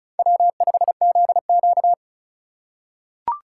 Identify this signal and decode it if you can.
Morse code, heard as a keyed tone, about 34 wpm, 700 Hz. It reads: W5ZQ
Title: USA Call Signs - 4 Characters - 4x Character Spacing - 35wpm